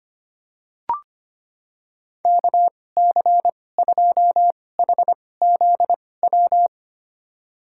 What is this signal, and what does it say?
Morse code, heard as a keyed tone, about 25 wpm, spaced as for 18 wpm, 700 Hz. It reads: KC2HZW
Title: USA Call Signs - 6 Characters - 2x Character Spacing - 25wpm